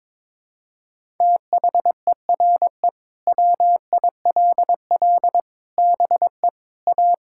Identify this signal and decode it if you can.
Morse code, heard as a keyed tone, about 22 wpm, 700 Hz. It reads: THERE WILL BE A
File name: practice / 22wpm / Sentences with 3-Word Phrases - 22wpm